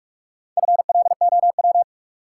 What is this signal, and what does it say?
Morse code, heard as a keyed tone, about 42 wpm, 700 Hz. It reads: FLOW
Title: Sentences from Top 1200 Words - New Word Drill Alt - 45wpm